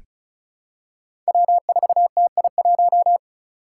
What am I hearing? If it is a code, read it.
Morse code, heard as a keyed tone, about 34 wpm, 700 Hz. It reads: W4TI1
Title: Five Challenging Letter-Number Combinations 35wpm